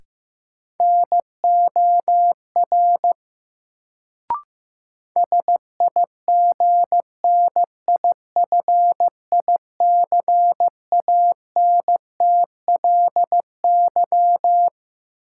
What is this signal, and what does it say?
Morse code, heard as a keyed tone, about 15 wpm, 700 Hz. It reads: NOR SIGNIFICANTLY
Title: Sentences from Top 1500 Words - New Word Drill - 15wpm